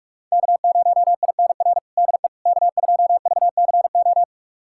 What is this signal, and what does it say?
Morse code, heard as a keyed tone, about 43 wpm, 700 Hz. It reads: K0INR DE K2VCO
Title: USA Callsign DE Callsign - 5 Characters - 45wpm